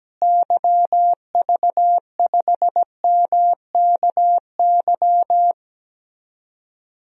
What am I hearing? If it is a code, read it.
Morse code, heard as a keyed tone, about 17 wpm, 700 Hz. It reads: YV5MKY
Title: DXCC-100 Call Signs - 6 Characters - Repeated 3x - 17wpm